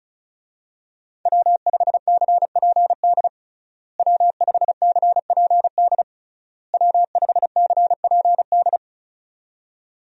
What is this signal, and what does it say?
Morse code, heard as a keyed tone, about 34 wpm, 700 Hz. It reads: W5CPD W5CPD W5CPD
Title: USA Call Signs - 5 Characters - Repeated 3x - 35wpm